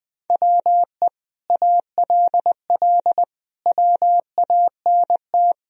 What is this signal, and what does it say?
Morse code, heard as a keyed tone, about 19 wpm, 700 Hz. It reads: WE ALL WANT
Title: Sentences from Top 1000 Words - Review Part 4 - 20wpm